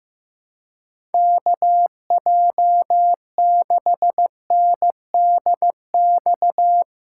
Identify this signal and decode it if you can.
Morse code, heard as a keyed tone, about 16 wpm, 700 Hz. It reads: KJ6NDX
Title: USA Call Signs - 6 Characters - 4x Character Spacing - 15wpm